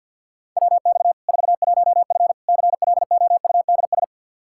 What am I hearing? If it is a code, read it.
Morse code, heard as a keyed tone, about 50 wpm, 700 Hz. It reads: WX 41F CLOUDS
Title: Practice QSO - Exchange 2 - Info - 50wpm